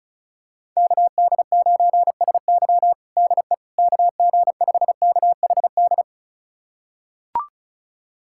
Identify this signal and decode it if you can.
Morse code, heard as a keyed tone, about 36 wpm, 700 Hz. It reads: KD9SY DE KG5KHD
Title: USA Callsign DE Callsign 35wpm